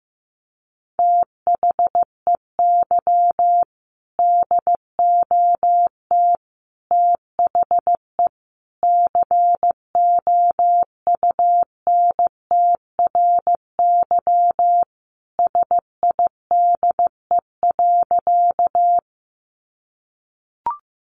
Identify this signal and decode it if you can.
Morse code, heard as a keyed tone, about 15 wpm, 700 Hz. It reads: THEY DOT THE COUNTRYSIDE.
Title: Sentences from Top 1500 Words 15wpm